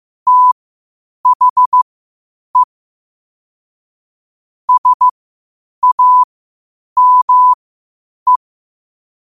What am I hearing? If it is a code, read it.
Morse code, heard as a keyed tone, about 15 wpm, spaced as for 8 wpm, 1000 Hz. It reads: THE SAME
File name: qotdfiles / 8wpm